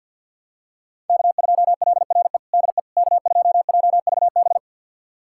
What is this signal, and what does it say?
Morse code, heard as a keyed tone, about 51 wpm, 700 Hz. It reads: K2LRE DE K2JVB
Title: USA Callsign DE Callsign - 5 Characters - 50wpm